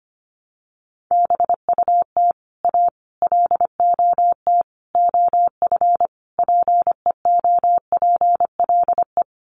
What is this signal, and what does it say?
Morse code, heard as a keyed tone, about 25 wpm, 700 Hz. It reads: BUT A LOT OF PEOPLE
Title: Sets of 5 Words Encore2 25wpm